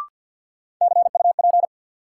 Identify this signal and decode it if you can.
Morse code, heard as a keyed tone, about 52 wpm, 700 Hz. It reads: CUP
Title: Sentences from Top 1600 Words - New Word Drill Alt - 50wpm